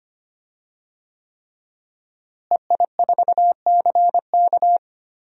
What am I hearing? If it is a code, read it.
Morse code, heard as a keyed tone, about 25 wpm, 700 Hz. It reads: EI4CK
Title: DXCC-100 Call Signs - 5 Characters - Repeated 3x - 25wpm